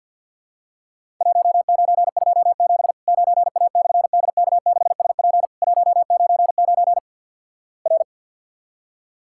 Morse code, heard as a keyed tone, about 50 wpm, 700 Hz. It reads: 1927 9A/DK6SP 199 R
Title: Worked All Europe DX Contest - QTC Exchange - 50wpm